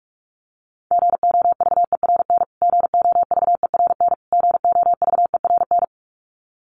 Morse code, heard as a keyed tone, about 44 wpm, 700 Hz. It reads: GOVERN GOVERN GOVERN
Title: Top 500 Words Repeated 45wpm